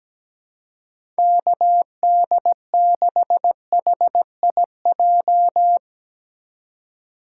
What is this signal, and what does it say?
Morse code, heard as a keyed tone, about 17 wpm, 700 Hz. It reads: KD6HIJ
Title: USA Call Signs - 6 Characters - 2x Character Spacing - 17wpm